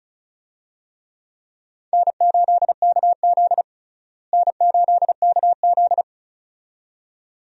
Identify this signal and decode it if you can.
Morse code, heard as a keyed tone, about 36 wpm, 700 Hz. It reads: N8KZ N8KZ
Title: USA Call Signs - 4 Characters - Repeated 2x - 35wpm